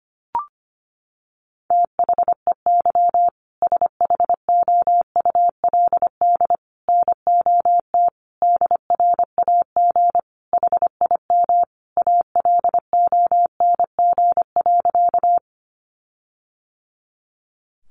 Morse code, heard as a keyed tone, about 25 wpm, 700 Hz. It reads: THEY SHOULD NOT DRAG HIM ALONG.
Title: Sentences from Top 1500 Words 25wpm